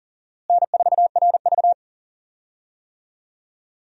Morse code, heard as a keyed tone, about 39 wpm, 700 Hz. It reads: N4RV
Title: USA Call Signs - 4 Characters - Repeated 3x - 40wpm